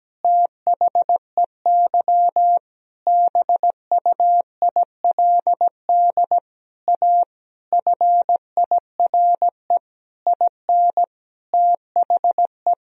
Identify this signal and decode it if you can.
Morse code, heard as a keyed tone, about 17 wpm, 700 Hz. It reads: THEY BUILD A FIRE IN THE
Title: Sentences from Top 1000 Words - Review Part 3 - 17wpm